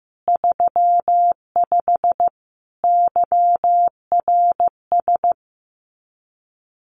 Morse code, heard as a keyed tone, about 15 wpm, 700 Hz. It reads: 35 YRS
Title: Common QSO Phrases Encore 15wpm